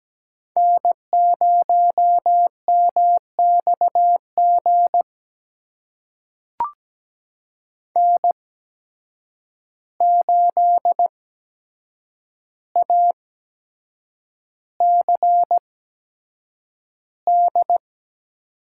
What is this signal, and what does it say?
Morse code, heard as a keyed tone, about 17 wpm, spaced as for 7 wpm, 700 Hz. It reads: N0MXG N8ACD
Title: USA Call Signs - 5 Characters - 8x Character Spacing - 17wpm